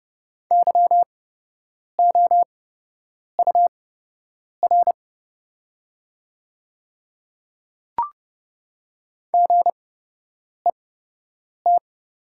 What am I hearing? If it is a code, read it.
Morse code, heard as a keyed tone, about 30 wpm, spaced as for 8 wpm, 700 Hz. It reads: YOUR GET
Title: 8x Character Spacing - Top 200 Words - 30wpm